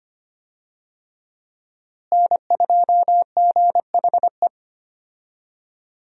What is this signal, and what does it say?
Morse code, heard as a keyed tone, about 25 wpm, 700 Hz. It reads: N2GHE
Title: USA Call Signs - 5 Characters - Repeated 3x - 25wpm